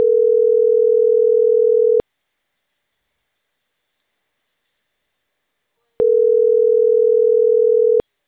Ringback tone (North America), 440 + 480 Hz.